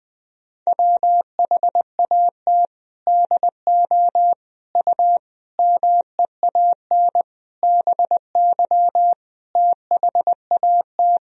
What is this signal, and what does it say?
Morse code, heard as a keyed tone, about 21 wpm, 700 Hz. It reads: WHAT DO U MEAN BY THAT
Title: Sentences from Top 400 Words and Abbreviations 20wpm